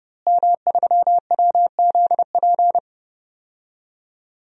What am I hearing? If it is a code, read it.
Morse code, heard as a keyed tone, about 30 wpm, 700 Hz. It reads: M3WZP
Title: DXCC-100 Call Signs - 5 Characters - Repeated 2x - 30wpm